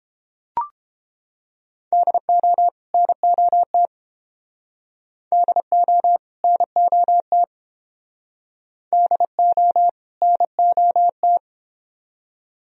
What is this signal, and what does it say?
Morse code, heard as a keyed tone, about 29 wpm, 700 Hz. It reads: DO NOT DO NOT DO NOT